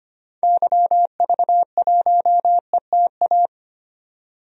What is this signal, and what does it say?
Morse code, heard as a keyed tone, about 25 wpm, 700 Hz. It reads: YV1ETA
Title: DXCC-100 Call Signs - 6 Characters - 25wpm